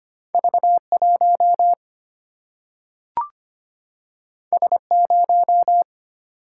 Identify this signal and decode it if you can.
Morse code, heard as a keyed tone, about 25 wpm, 700 Hz. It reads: V1 S0
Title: Single-Letter Single-Number 25wpm